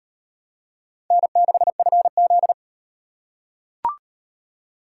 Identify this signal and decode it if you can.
Morse code, heard as a keyed tone, about 38 wpm, 700 Hz. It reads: N6FZ